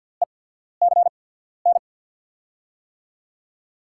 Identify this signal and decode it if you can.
Morse code, heard as a keyed tone, about 52 wpm, spaced as for 14 wpm, 700 Hz. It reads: ECN
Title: USA Call Signs - 6 Characters - 8x Character Spacing - 50wpm